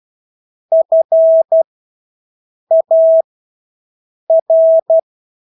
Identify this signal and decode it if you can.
Morse code, heard as a keyed tone, about 12 wpm, spaced as for 6 wpm, 650 Hz. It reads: FAR
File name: Engelske ord 12-6 wpm